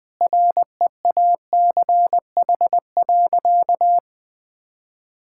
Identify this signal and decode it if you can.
Morse code, heard as a keyed tone, about 21 wpm, 700 Hz. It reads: REACH.
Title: Sentences with 2-Word Phrases - 20wpm